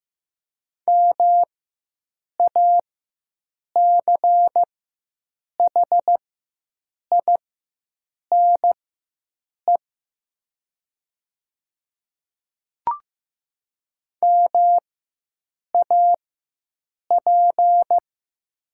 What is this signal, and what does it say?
Morse code, heard as a keyed tone, about 15 wpm, spaced as for 7 wpm, 700 Hz. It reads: MACHINE MAP
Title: 4x Character Spacing - Top 500 Words - 15wpm